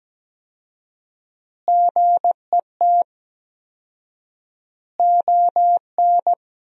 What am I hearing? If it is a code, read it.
Morse code, heard as a keyed tone, about 17 wpm, 700 Hz. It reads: GET ON
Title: 4x Word-Spacing - Sentences from Top 200 Words - 17wpm